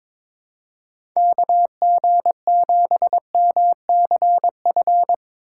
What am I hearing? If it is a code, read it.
Morse code, heard as a keyed tone, about 22 wpm, 700 Hz. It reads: KG7MCF
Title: USA Call Signs - 6 Characters - 4x Character Spacing - 22wpm